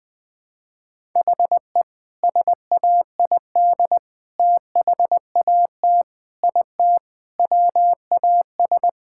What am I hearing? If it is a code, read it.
Morse code, heard as a keyed tone, about 21 wpm, 700 Hz. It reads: HE SAID THAT IT WAS
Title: Sentences with 2-Word Phrases - Spoken Beginning - 20wpm